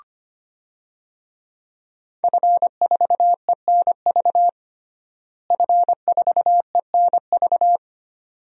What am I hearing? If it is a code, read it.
Morse code, heard as a keyed tone, about 25 wpm, 700 Hz. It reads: F4ENV F4ENV
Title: DXCC-100 Call Signs - 5 Characters - Repeated 2x - 25wpm